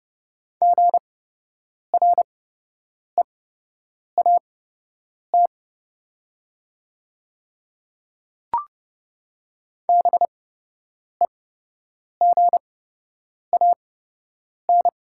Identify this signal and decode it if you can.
Morse code, heard as a keyed tone, about 30 wpm, spaced as for 8 wpm, 700 Hz. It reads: GREAT BEGAN
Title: 8x Character Spacing - Top 300 Words - 30wpm